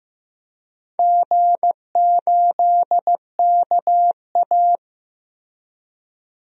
Morse code, heard as a keyed tone, about 15 wpm, 700 Hz. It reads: G8KA
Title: DXCC-100 Call Signs - 4 Characters - Repeated 2x - 15wpm